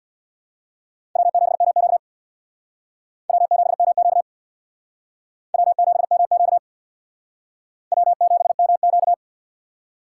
Morse code, heard as a keyed tone, about 61 wpm, 700 Hz. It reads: W7MQ W7MQ W7MQ W7MQ
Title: Speed Racing - USA 1x2 Call Signs 50wpm